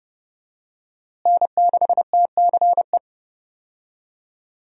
Morse code, heard as a keyed tone, about 32 wpm, 700 Hz. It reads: N6TCE
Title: USA Call Signs - 5 Characters - Repeated 3x - 30wpm